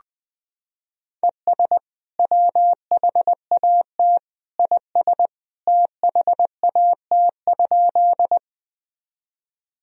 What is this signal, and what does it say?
Morse code, heard as a keyed tone, about 20 wpm, 700 Hz. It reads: ES WHAT IS THAT?
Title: Sentences from Top 100 Words and Abbreviations 20wpm